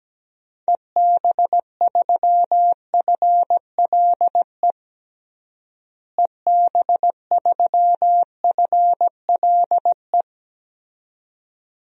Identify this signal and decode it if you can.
Morse code, heard as a keyed tone, about 17 wpm, 700 Hz. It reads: EB3FLE EB3FLE
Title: DXCC-100 Call Signs - 6 Characters - Repeated 2x - 17wpm